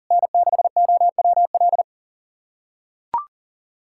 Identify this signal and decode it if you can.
Morse code, heard as a keyed tone, about 40 wpm, 700 Hz. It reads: N6OWL
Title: USA Callsign DE Callsign - 5 Characters - 40wpm